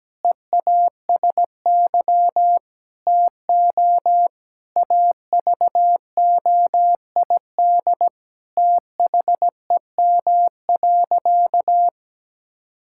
Morse code, heard as a keyed tone, about 17 wpm, 700 Hz. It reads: EASY TO AVOID THEM.
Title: Sentences from Top 1000 Words - Review Part 6 - 17wpm